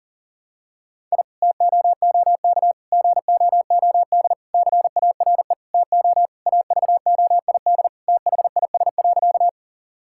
Morse code, heard as a keyed tone, about 40 wpm, 700 Hz. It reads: I TOOK GOOD CARE TO AVOID THIS.
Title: Sentences from Top 1000 Words - Review Part 6 - 40wpm